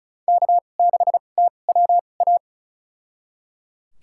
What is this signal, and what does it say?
Morse code, heard as a keyed tone, about 34 wpm, spaced as for 25 wpm, 700 Hz. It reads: K6TWA
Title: USA Call Signs - 2x Character Spacing - 35wpm